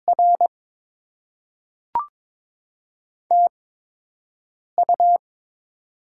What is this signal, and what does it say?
Morse code, heard as a keyed tone, about 22 wpm, spaced as for 6 wpm, 700 Hz. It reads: R TU
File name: Top 20 QSO Elements - 8x Character Spacing - 22wpm